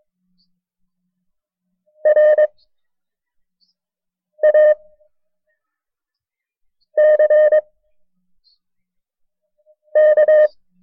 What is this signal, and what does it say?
Morse code, heard as a keyed tone, about 22 wpm, spaced as for 4 wpm, 610 Hz. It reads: RACK
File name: FreeFiles / CW-practice / KB6MT-tape1-side2